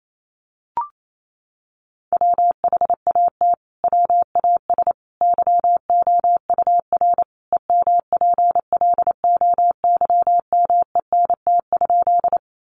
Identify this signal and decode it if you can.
Morse code, heard as a keyed tone, about 28 wpm, 700 Hz. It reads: WHAT WAS YOUR EMPLOYMENT?
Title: Sentences from Top 1500 Words 28wpm